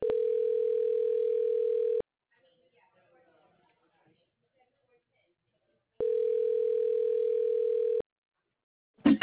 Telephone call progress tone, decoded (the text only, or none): ringback tone